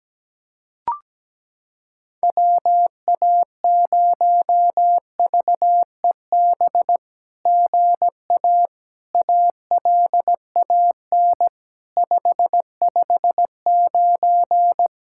Morse code, heard as a keyed tone, about 17 wpm, 700 Hz. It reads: WA0VEB GA ALAN 559
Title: SOTA QSO Practice 17wpm